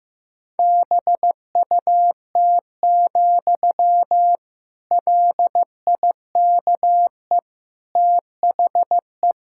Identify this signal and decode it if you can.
Morse code, heard as a keyed tone, about 16 wpm, 700 Hz. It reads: BUT, LIKE THE
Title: Sentences from Top 1600 Words 15wpm